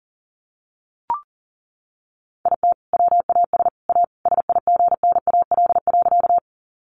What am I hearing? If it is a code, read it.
Morse code, heard as a keyed tone, about 40 wpm, 700 Hz. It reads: IT WAS A SIGNAL.
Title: Sentences from Top 1400 Words 40wpm